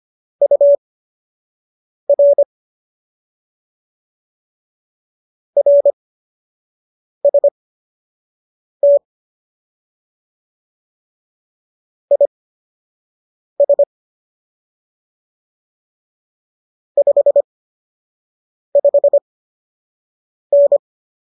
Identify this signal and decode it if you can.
Morse code, heard as a keyed tone, about 25 wpm, spaced as for 6 wpm, 570 Hz. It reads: UR RST IS 55N